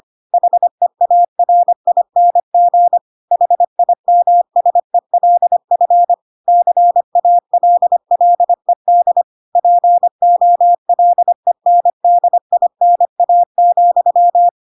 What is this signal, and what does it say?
Morse code, heard as a keyed tone, about 25 wpm, 700 Hz. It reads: HEARING HIMSELF CALLED POLENDINA,
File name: Pinocchio / Pinocchio ch2 25 25 ik1hge